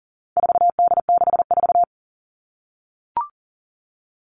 Morse code, heard as a keyed tone, about 40 wpm, 700 Hz. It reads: 4D64
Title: All Cut Numbers - 4-Digit 40wpm